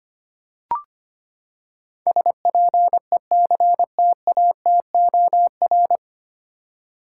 Morse code, heard as a keyed tone, about 25 wpm, 700 Hz. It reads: SPECTATOR
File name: Nine Letter Words - 25wpm